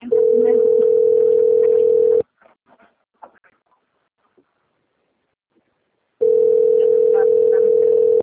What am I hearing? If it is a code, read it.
Ringback tone (North America), 440 + 480 Hz.